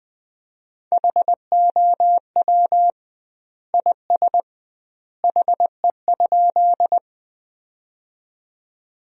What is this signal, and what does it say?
Morse code, heard as a keyed tone, about 21 wpm, 700 Hz. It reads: HOW IS HE?
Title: 2x Word-Spacing - Sentences from Top 100 Words - 20wpm